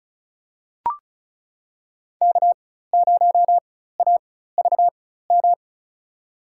Morse code, heard as a keyed tone, about 34 wpm, spaced as for 16 wpm, 700 Hz. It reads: K0AVM
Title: USA Call Signs - 5 Characters - 4x Character Spacing - 35wpm